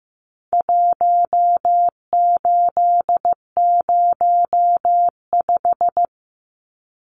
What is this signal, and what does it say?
Morse code, heard as a keyed tone, about 15 wpm, 700 Hz. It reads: 1805
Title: Numbers - Contest Style - 4-Digit 15wpm